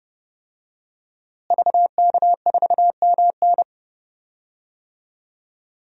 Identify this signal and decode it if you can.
Morse code, heard as a keyed tone, about 30 wpm, 700 Hz. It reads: VK4MN
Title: DXCC-100 Call Signs - 5 Characters - Repeated 2x - 30wpm